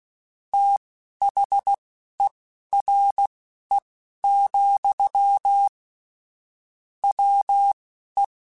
Morse code, heard as a keyed tone, about 16 wpm, spaced as for 11 wpm, 800 Hz. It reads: THERE, WE
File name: Technical Subjects / Technical Subjects-mp3s / Sw Pwr Sup @10